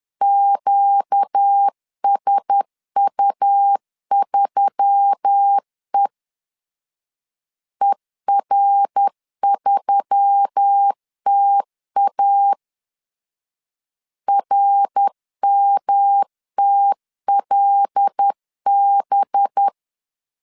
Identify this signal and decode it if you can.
Morse code, heard as a keyed tone, about 10 wpm, 800 Hz. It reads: QSU3E ER3TA RMTLB
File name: LEZIONI / 6 PKX0 / 6 PKX 10wpm 2426